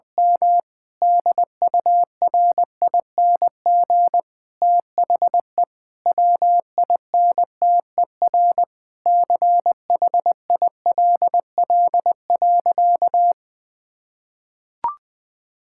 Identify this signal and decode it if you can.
Morse code, heard as a keyed tone, about 20 wpm, 700 Hz. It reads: M DURING THE WINTER CHILL.
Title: Difficult Sentences with Hints - 20wpm